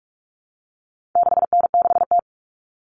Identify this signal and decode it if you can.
Morse code, heard as a keyed tone, about 44 wpm, 700 Hz. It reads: 6N6T
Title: All Cut Numbers - 4-Digit 45wpm